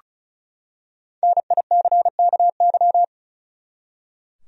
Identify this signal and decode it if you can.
Morse code, heard as a keyed tone, about 36 wpm, 700 Hz. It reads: NICKY
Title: USA Common Names - 35wpm